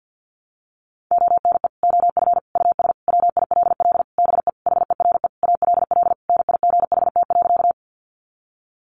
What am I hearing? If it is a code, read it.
Morse code, heard as a keyed tone, about 50 wpm, 700 Hz. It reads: ONE OF US WILL BE HERE ALL NIGHT.
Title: Sentences from Top 300 Words 50wpm